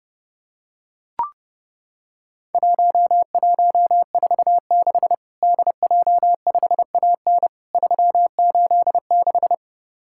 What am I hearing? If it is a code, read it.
Morse code, heard as a keyed tone, about 30 wpm, 700 Hz. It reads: 1146 DJ5AN 386
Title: Worked All Europe DX Contest - QTC Exchange - 30wpm